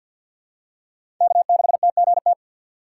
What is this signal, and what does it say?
Morse code, heard as a keyed tone, about 52 wpm, 700 Hz. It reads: K6TGT
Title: USA Call Signs - 5 Characters - 3x Character Spacing - 50wpm